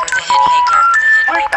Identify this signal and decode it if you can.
Special information tone, with speech.